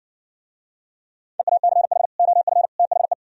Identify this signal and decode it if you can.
Morse code, heard as a keyed tone, about 59 wpm, 700 Hz. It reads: EACH OF THE